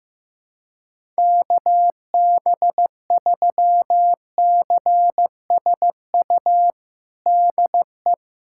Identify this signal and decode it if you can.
Morse code, heard as a keyed tone, about 15 wpm, 700 Hz. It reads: KB3CSU DE
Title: USA Callsign DE Callsign - 6 Characters - 15wpm